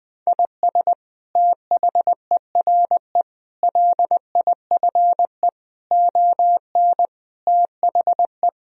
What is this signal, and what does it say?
Morse code, heard as a keyed tone, about 20 wpm, 700 Hz. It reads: IS THERE LIFE ON THE
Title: Sentences from Top 1000 Words - Review Part 3 - 20wpm